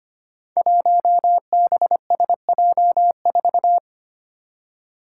Morse code, heard as a keyed tone, about 25 wpm, 700 Hz. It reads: 1BSJ4
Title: Five Challenging Letter-Number Combinations 25wpm